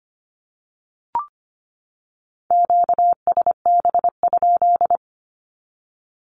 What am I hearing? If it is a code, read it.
Morse code, heard as a keyed tone, about 25 wpm, 700 Hz. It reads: QSB?
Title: Q-Codes 25wpm